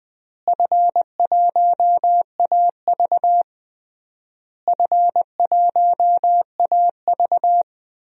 Morse code, heard as a keyed tone, about 21 wpm, 700 Hz. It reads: F1AV F1AV
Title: DXCC-100 Call Signs - 4 Characters - Repeated 3x - 20wpm